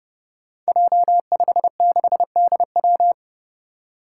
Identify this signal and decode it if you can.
Morse code, heard as a keyed tone, about 30 wpm, 700 Hz. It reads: J56DW
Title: Five Challenging Letter-Number Combinations 30wpm